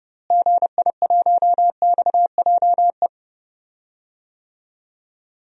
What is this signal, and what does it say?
Morse code, heard as a keyed tone, about 30 wpm, 700 Hz. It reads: GI1XJE
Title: DXCC-100 Call Signs - 6 Characters - Repeated 2x - 30wpm